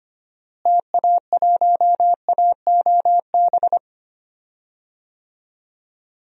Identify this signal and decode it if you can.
Morse code, heard as a keyed tone, about 25 wpm, 700 Hz. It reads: TA1AOB
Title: DXCC-100 Call Signs - 6 Characters - Repeated 2x - 25wpm